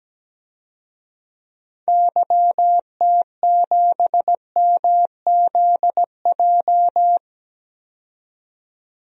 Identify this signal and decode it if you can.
Morse code, heard as a keyed tone, about 17 wpm, 700 Hz. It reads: YT7MZJ
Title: DXCC-100 Call Signs - 6 Characters - 17wpm